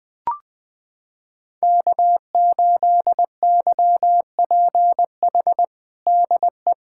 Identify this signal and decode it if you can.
Morse code, heard as a keyed tone, about 21 wpm, 700 Hz. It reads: K8YPH DE
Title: USA Callsign DE Callsign - 5 Characters - 20wpm